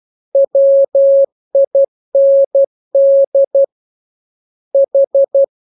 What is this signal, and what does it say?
Morse code, heard as a keyed tone, about 12 wpm, 550 Hz. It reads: WIND H
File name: jokes / joke 5 12 0000